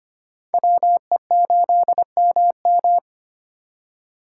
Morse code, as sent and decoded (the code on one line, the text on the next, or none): .-- . ---.. -- --
WE8MM